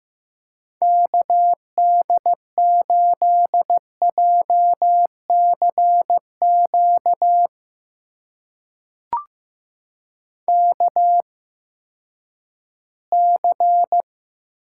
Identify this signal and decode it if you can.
Morse code, heard as a keyed tone, about 16 wpm, spaced as for 11 wpm, 700 Hz. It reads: KD8JCQ KC
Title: USA Call Signs - 6 Characters - 8x Character Spacing - 15wpm